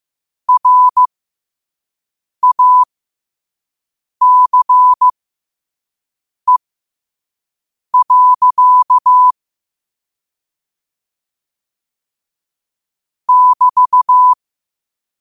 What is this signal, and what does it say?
Morse code, heard as a keyed tone, about 15 wpm, spaced as for 5 wpm, 1000 Hz. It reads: RACE. <BT>
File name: qotdfiles / 5wpm